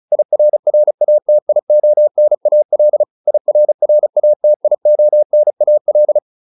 Morse code, heard as a keyed tone, about 34 wpm, 600 Hz. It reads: IRRATIONAL IRRATIONAL
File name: IR8 35